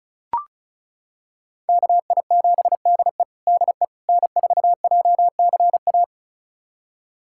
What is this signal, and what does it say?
Morse code, heard as a keyed tone, about 36 wpm, 700 Hz. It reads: KI7DE DE N4JCA
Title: USA Callsign DE Callsign - 5 Characters - 35wpm